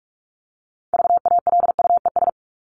Morse code, heard as a keyed tone, about 44 wpm, 700 Hz. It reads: VALUES
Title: ES:S Suffix - Top 100 Words with Base Words 45wpm